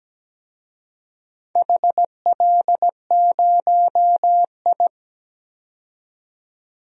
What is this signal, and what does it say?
Morse code, heard as a keyed tone, about 17 wpm, 700 Hz. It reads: HL0I
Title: DXCC-100 Call Signs - 4 Characters - 17wpm